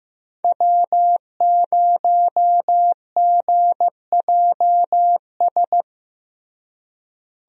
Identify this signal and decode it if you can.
Morse code, heard as a keyed tone, about 15 wpm, 700 Hz. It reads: W0GJS
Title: USA Call Signs - 5 Characters - Repeated 3x - 15wpm